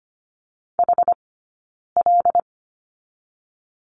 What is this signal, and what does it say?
Morse code, heard as a keyed tone, about 25 wpm, 700 Hz. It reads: H L
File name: Lesson 15d - Number 5 - Two Characters 25wpm